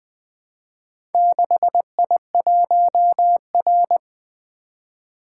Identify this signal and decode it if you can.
Morse code, heard as a keyed tone, about 20 wpm, 700 Hz. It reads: 6I1R
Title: Four Letter-Number Combinations - 20wpm